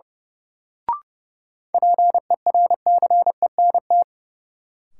Morse code, heard as a keyed tone, about 30 wpm, 700 Hz. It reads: PERCENT
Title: Sets of 2 Words - New Word Drill - 30wpm